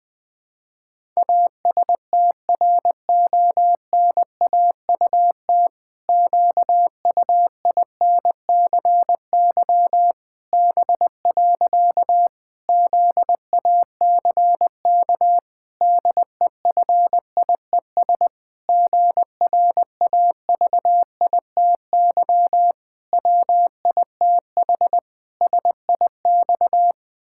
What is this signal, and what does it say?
Morse code, heard as a keyed tone, about 20 wpm, 700 Hz. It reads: ASTRONAUT QUINCY B. ZACK DEFIES GRAVITY WITH SIX
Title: Pangram Challenge v2 - 20wpm